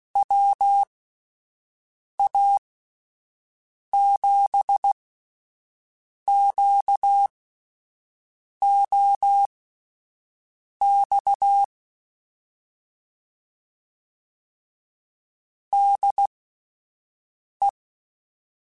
Morse code, heard as a keyed tone, about 16 wpm, spaced as for 5 wpm, 800 Hz. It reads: WA7QOX DE
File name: (spaced) / QSOs Spaced-mp3s / WA7QOX 2 @10 sp